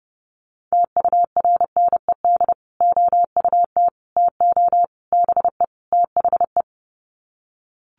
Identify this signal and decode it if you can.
Morse code, heard as a keyed tone, about 28 wpm, 700 Hz. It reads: TURNED OUT TO BE THE
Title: Sets of 5 Words Encore 30wpm